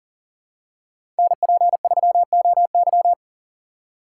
Morse code, heard as a keyed tone, about 40 wpm, 700 Hz. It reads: NP3OY
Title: USA Call Signs - 3x Character Spacing - 40wpm